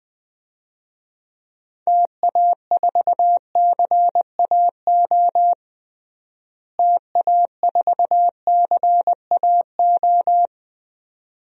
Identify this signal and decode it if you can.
Morse code, heard as a keyed tone, about 20 wpm, 700 Hz. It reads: TA4CAO TA4CAO
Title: DXCC-100 Call Signs - 6 Characters - Repeated 2x - 20wpm